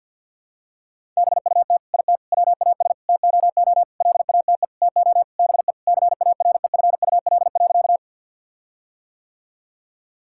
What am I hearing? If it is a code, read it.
Morse code, heard as a keyed tone, about 50 wpm, 700 Hz. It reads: BUT IT WAS TOO LATE TO BE CAREFUL.
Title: Sentences from Top 1600 Words 50wpm